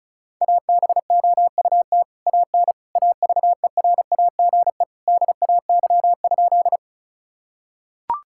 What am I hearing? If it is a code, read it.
Morse code, heard as a keyed tone, about 36 wpm, 700 Hz. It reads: ABOUT AN AVERAGE DAY?
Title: Sentences from Top 1000 Words - Review Part 6 - 35wpm